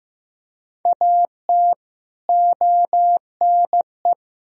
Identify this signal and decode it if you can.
Morse code, heard as a keyed tone, about 16 wpm, 700 Hz. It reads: AT ONE
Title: Sentences from Top 1000 Words - Review Part 1 - 15wpm